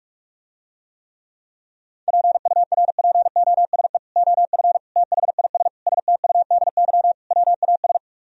Morse code, heard as a keyed tone, about 49 wpm, 700 Hz. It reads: PURPOSE OF THIS STUDY WAS